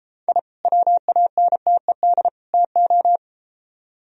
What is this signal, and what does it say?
Morse code, heard as a keyed tone, about 32 wpm, 700 Hz. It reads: I WANTED TO